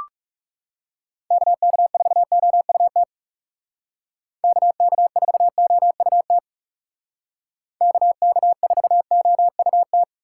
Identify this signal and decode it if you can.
Morse code, heard as a keyed tone, about 40 wpm, 700 Hz. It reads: KK4OUT KK4OUT KK4OUT